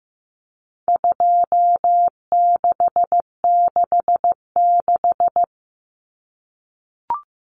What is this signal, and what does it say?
Morse code, heard as a keyed tone, about 15 wpm, 700 Hz. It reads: 2666
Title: Common Cut Numbers - 4-Digit 15wpm